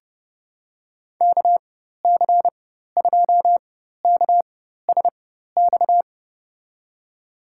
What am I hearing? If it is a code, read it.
Morse code, heard as a keyed tone, about 30 wpm, spaced as for 14 wpm, 700 Hz. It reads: KC2KSX
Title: USA Call Signs - 6 Characters - 4x Character Spacing - 30wpm